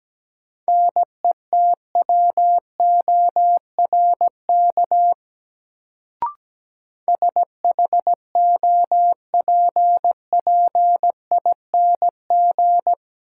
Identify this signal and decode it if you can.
Morse code, heard as a keyed tone, about 17 wpm, 700 Hz. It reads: NETWORK SHOPPING
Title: Sentences from Top 1000 Words - Review Part 7 - New Word Drill - 17wpm